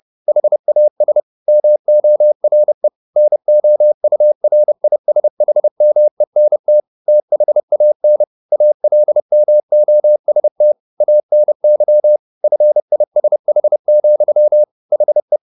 Morse code, heard as a keyed tone, about 30 wpm, 600 Hz. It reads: HAS MORE NOURISHMENT THAN ALMOST ANY FISH, HE